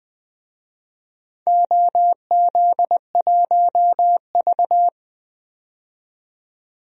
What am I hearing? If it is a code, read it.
Morse code, heard as a keyed tone, about 20 wpm, 700 Hz. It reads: OZ1V
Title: DXCC-100 Call Signs - 4 Characters - 20wpm